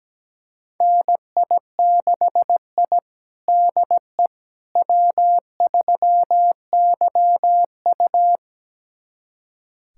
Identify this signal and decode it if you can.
Morse code, heard as a keyed tone, about 17 wpm, 700 Hz. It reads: NI6I DE W3YU
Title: USA Callsign DE Callsign - 4 Characters - 17wpm